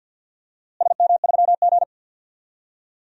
Morse code, heard as a keyed tone, about 46 wpm, 700 Hz. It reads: SM3G
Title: DXCC-100 Call Signs - 4 Characters - Repeated 2x - 50wpm